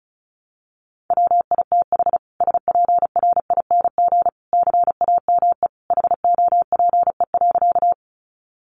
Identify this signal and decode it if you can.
Morse code, heard as a keyed tone, about 34 wpm, 700 Hz. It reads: WITH SPRING CAME HOPE.
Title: Sentences from Top 1200 Words 35wpm